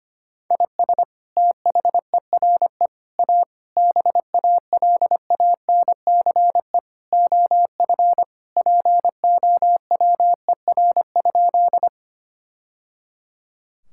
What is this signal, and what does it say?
Morse code, heard as a keyed tone, about 25 wpm, 700 Hz. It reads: IS THERE A BALANCE OF POWER?
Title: Sentences from Top 1000 Words - Review Part 6 - 25wpm